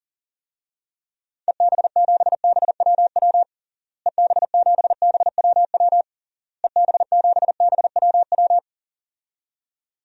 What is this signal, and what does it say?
Morse code, heard as a keyed tone, about 40 wpm, 700 Hz. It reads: EB7BWW EB7BWW EB7BWW
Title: DXCC-100 Call Signs - 6 Characters - Repeated 3x - 40wpm